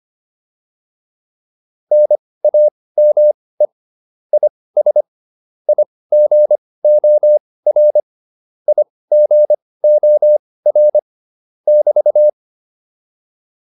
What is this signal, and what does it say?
Morse code, heard as a keyed tone, about 25 wpm, spaced as for 18 wpm, 600 Hz. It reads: NAME IS IGOR IGOR <BT>
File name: QSO204 18